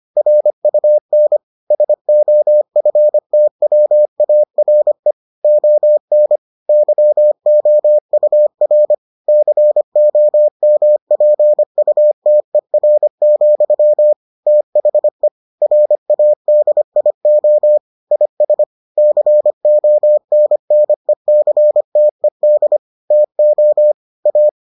Morse code, heard as a keyed tone, about 25 wpm, 600 Hz. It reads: RUN SOFTWARE ON YOUR COMPUTER, THE RADIO IS CONNECTED TO A